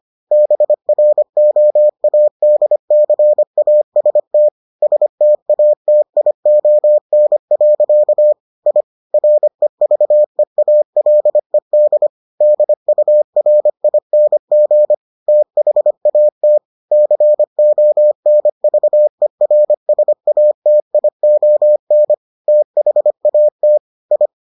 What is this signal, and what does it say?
Morse code, heard as a keyed tone, about 25 wpm, 600 Hz. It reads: BROADCAST STATION. I REVEALED DURING THAT CONVERSATION THAT I